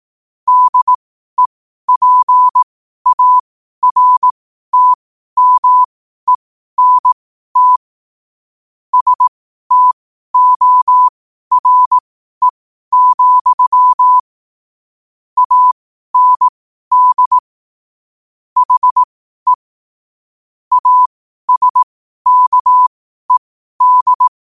Morse code, heard as a keyed tone, about 18 wpm, spaced as for 12 wpm, 1000 Hz. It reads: DEPARTMENT STORE, AND HE ASKED